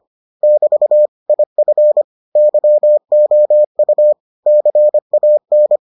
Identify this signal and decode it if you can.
Morse code, heard as a keyed tone, about 25 wpm, 600 Hz. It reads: <BT> IF YOU CAN